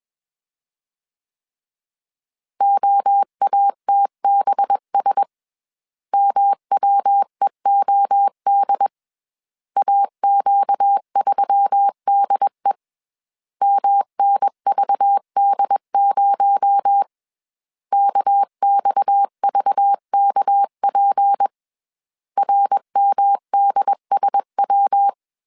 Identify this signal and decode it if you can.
Morse code, heard as a keyed tone, about 21 wpm, 800 Hz. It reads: OATBS MWEOD AQ3DE MNVD0 KXVKP RMDSW
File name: CW / DET / 1 6 1500C S C 20wpm 2116